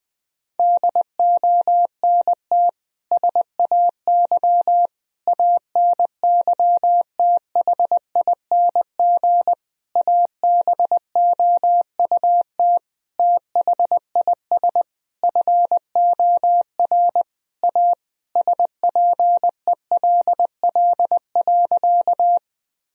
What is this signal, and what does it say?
Morse code, heard as a keyed tone, about 20 wpm, 700 Hz. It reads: DONT SAY ANYTHING ABOUT THIS FOR A SPELL.
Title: Sentences from Top 1000 Words - Review Part 1 - 20wpm